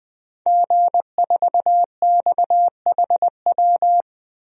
Morse code, heard as a keyed tone, about 21 wpm, 700 Hz. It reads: G4XHW
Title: DXCC-100 Call Signs - 5 Characters - Repeated 2x - 20wpm